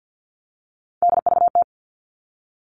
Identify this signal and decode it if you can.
Morse code, heard as a keyed tone, about 50 wpm, 700 Hz. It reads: DVT
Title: All Cut Numbers - 3-Digit 50wpm